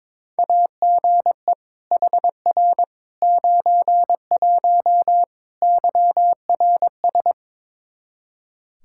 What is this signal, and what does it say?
Morse code, heard as a keyed tone, about 22 wpm, 700 Hz. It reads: AGE HR 91 YRS
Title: Practice QSO - Exchange 3 - More Info - 22wpm